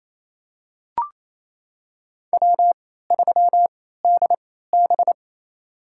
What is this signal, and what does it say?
Morse code, heard as a keyed tone, about 28 wpm, spaced as for 16 wpm, 700 Hz. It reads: W3DB
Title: USA Call Signs - 3x Character Spacing - 28wpm